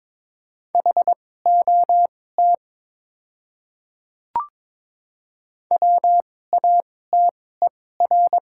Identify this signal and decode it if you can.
Morse code, heard as a keyed tone, about 22 wpm, spaced as for 16 wpm, 700 Hz. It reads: HOT WATER
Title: 2x Character Spacing - Top 100 Words - 22wpm